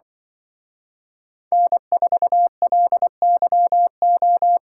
Morse code, heard as a keyed tone, about 24 wpm, 700 Hz. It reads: N4LYO
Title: Speed Racing - USA 1x3 Call Signs 20wpm